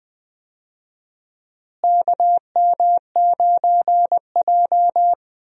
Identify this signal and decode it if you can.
Morse code, heard as a keyed tone, about 20 wpm, 700 Hz. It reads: KM9J
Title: USA Call Signs - 4 Characters - Repeated 3x - 20wpm